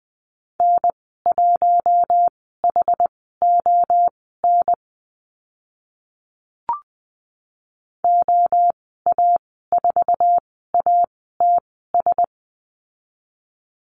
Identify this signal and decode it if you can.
Morse code, heard as a keyed tone, about 20 wpm, spaced as for 15 wpm, 700 Hz. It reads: N1HON OA4ATS
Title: 082 Lesson 14g - Number 2 - Call Signs 20wpm